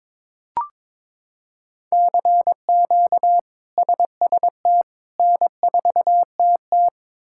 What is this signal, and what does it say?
Morse code, heard as a keyed tone, about 22 wpm, 700 Hz. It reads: CQ SST N4TT
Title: K1USN SST QSO Practice - 22wpm